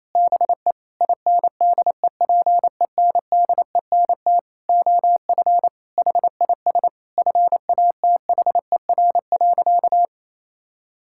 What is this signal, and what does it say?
Morse code, heard as a keyed tone, about 27 wpm, 700 Hz. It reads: BE INDEPENDENT OF HIS FATHER.
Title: Sentences from Top 1000 Words - Review Part 6 - 28wpm